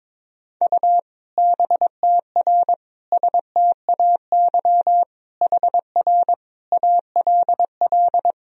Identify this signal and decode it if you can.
Morse code, heard as a keyed tone, about 22 wpm, 700 Hz. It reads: U BTR STAY HR ALL
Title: Sentences from Top 500 Words and Abbreviations 22wpm